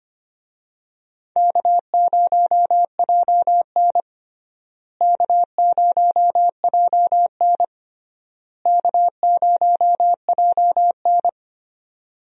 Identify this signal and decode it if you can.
Morse code, heard as a keyed tone, about 25 wpm, 700 Hz. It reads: K0JN K0JN K0JN
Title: USA Call Signs - 4 Characters - Repeated 3x - 25wpm